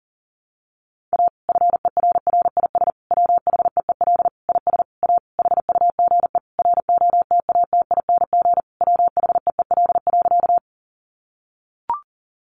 Morse code, heard as a keyed tone, about 40 wpm, 700 Hz. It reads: A FERRIS WHEEL IS A HUGE ROTATING WHEEL.
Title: Additional Easy Phrases 40wpm